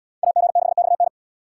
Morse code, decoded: AG6XN

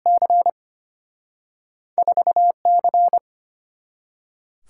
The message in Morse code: C 4 C

